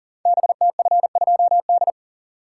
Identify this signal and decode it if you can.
Morse code, heard as a keyed tone, about 38 wpm, 700 Hz. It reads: BTF2D
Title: Code Groups - 5 Letter:Numbers - 40wpm